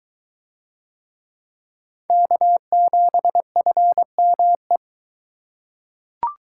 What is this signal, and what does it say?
Morse code, heard as a keyed tone, about 23 wpm, 700 Hz. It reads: K7FME